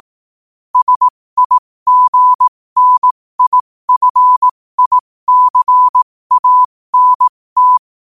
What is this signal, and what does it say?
Morse code, heard as a keyed tone, about 18 wpm, spaced as for 15 wpm, 1000 Hz. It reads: SIGNIFICANT
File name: qotdfiles / 15wpm